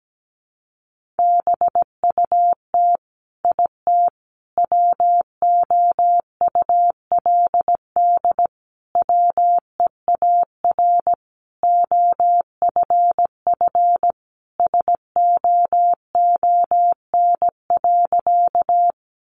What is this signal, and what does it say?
Morse code, heard as a keyed tone, about 17 wpm, 700 Hz. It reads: BUT IT WOULD WEAR OFF SOON.
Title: Sentences from Top 1500 Words 17wpm